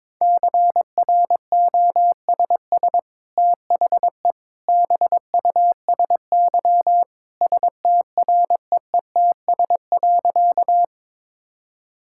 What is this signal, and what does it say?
Morse code, heard as a keyed tone, about 22 wpm, 700 Hz. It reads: CROSS THE BUSY STREETS.
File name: practice / 22wpm / Easy Sentences with Hints - 22wpm